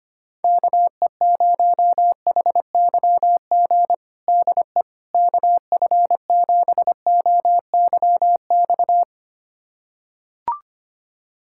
Morse code, heard as a keyed tone, about 25 wpm, 700 Hz. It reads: KE0HYG DE KF7OYX
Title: USA Callsign DE Callsign - 6 Characters - 25wpm